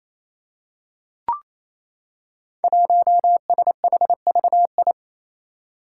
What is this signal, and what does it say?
Morse code, heard as a keyed tone, about 28 wpm, 700 Hz. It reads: 1SHVI
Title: Five Challenging Letter-Number Combinations 28wpm